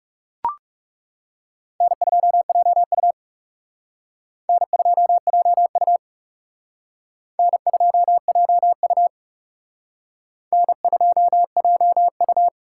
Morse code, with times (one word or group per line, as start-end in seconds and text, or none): 1.8-3.1 N2JU
4.5-6.0 N2JU
7.4-9.1 N2JU
10.5-12.5 N2JU